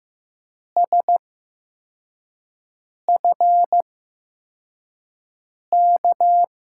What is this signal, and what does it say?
Morse code, heard as a keyed tone, about 15 wpm, spaced as for 4 wpm, 700 Hz. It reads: SFK
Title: USA Call Signs - 6 Characters - 8x Character Spacing - 15wpm